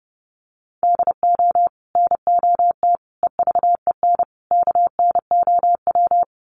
Morse code, heard as a keyed tone, about 28 wpm, 700 Hz. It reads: DO NOT EVEN KNOW
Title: Sets of 4 Words 30wpm